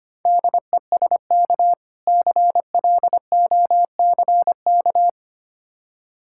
Morse code, heard as a keyed tone, about 25 wpm, 700 Hz. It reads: DESK CLOCK
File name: Easy Words - Speed-Racing with Hints - 25wpm